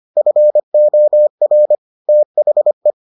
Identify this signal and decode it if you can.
Morse code, heard as a keyed tone, about 25 wpm, 600 Hz. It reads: FOR THE